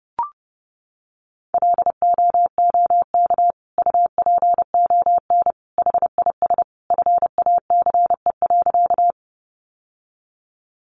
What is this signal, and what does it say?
Morse code, heard as a keyed tone, about 30 wpm, 700 Hz. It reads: LOOK UPON HIS FACE.
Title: Sentences from Top 1100 Words 30wpm